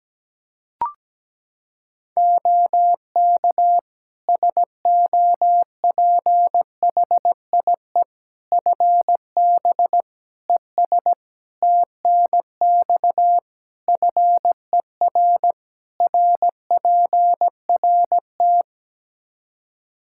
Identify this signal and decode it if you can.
Morse code, heard as a keyed tone, about 17 wpm, 700 Hz. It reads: OK SOPHIE FB ES TNX FER RPRT
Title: Practice QSO - Exchange 2 - Info - 17wpm